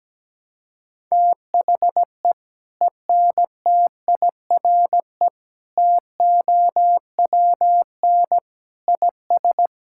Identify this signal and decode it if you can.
Morse code, heard as a keyed tone, about 17 wpm, 700 Hz. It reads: THE ENTIRE TOWN IS
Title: Sentences from Top 1000 Words - Review Part 5 - 17wpm